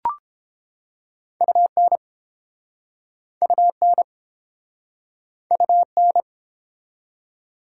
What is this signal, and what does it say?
Morse code, heard as a keyed tone, about 29 wpm, 700 Hz. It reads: UN UN UN